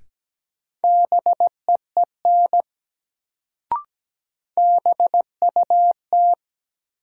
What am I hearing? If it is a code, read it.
Morse code, heard as a keyed tone, about 17 wpm, 700 Hz. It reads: BEEN BUT
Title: Sets of 2 Words - New Word Drill - 17wpm